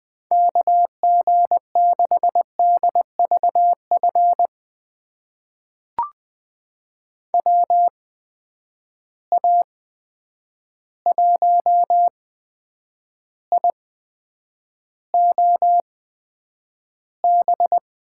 Morse code, heard as a keyed tone, about 20 wpm, spaced as for 9 wpm, 700 Hz. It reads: KG6DVF WA1IOB
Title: USA Call Signs - 6 Characters - 8x Character Spacing - 20wpm